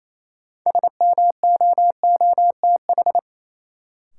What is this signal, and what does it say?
Morse code, heard as a keyed tone, about 28 wpm, 700 Hz. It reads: SMOOTH
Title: Sentences from Top 1400 Words - New Word Drill - 28wpm